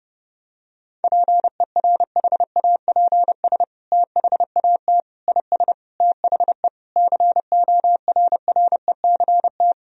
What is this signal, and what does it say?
Morse code, heard as a keyed tone, about 30 wpm, 700 Hz. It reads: PERHAPS THAT IS THE CORRECT
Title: Sentences from Top 1000 Words - Review Part 4 - 30wpm